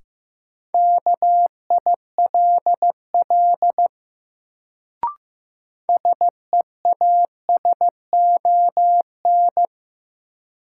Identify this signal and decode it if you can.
Morse code, heard as a keyed tone, about 15 wpm, 700 Hz. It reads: KILL SEASON
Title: Sentences from Top 1400 Words - New Word Drill Alt - 15wpm